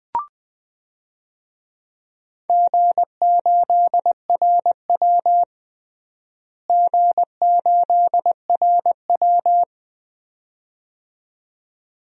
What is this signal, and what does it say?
Morse code, heard as a keyed tone, about 20 wpm, 700 Hz. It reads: G8RW G8RW
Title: DXCC-100 Call Signs - 4 Characters - Repeated 2x - 20wpm